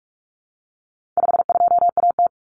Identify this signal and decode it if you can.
Morse code, heard as a keyed tone, about 44 wpm, 700 Hz. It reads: 52AT
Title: Common Cut Numbers - 4-Digit 45wpm